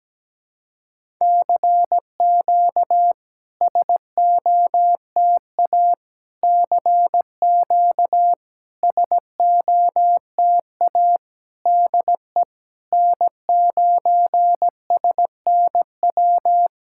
Morse code, heard as a keyed tone, about 17 wpm, 700 Hz. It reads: CQ SOTA CQ SOTA DE N9SNW
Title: SOTA QSO Practice 17wpm